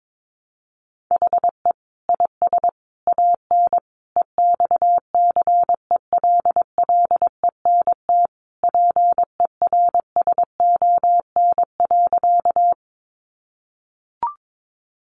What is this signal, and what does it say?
Morse code, heard as a keyed tone, about 22 wpm, 700 Hz. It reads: HE IS AN EXCELLENT PERSON.
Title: Sentences from Top 1000 Words 22wpm